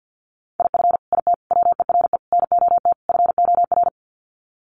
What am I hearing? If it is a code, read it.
Morse code, heard as a keyed tone, about 50 wpm, 700 Hz. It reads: IF IT WERE NOT FOR
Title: Sets of 5 Words 50wpm